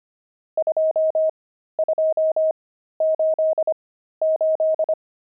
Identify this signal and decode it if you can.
Morse code, heard as a keyed tone, about 25 wpm, spaced as for 13 wpm, 630 Hz. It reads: 2288
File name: pangram-fs-13-cs-25-nums10